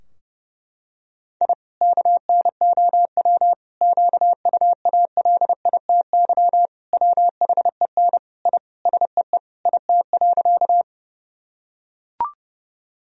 Morse code, heard as a keyed tone, about 32 wpm, 700 Hz. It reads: I KNOW QUALITY WHEN I SEE IT.
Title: Sentences from Top 1000 Words - Review Part 4 - 30wpm